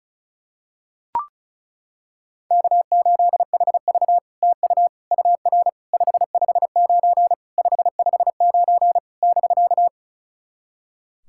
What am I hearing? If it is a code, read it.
Morse code, heard as a keyed tone, about 36 wpm, 700 Hz. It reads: K8HV TU UR 559 559 <BK>
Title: SOTA QSO Practice 35wpm